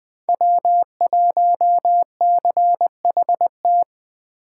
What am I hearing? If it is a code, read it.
Morse code, heard as a keyed tone, about 24 wpm, 700 Hz. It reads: W1CHT
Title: Speed Racing - USA Call Signs 20wpm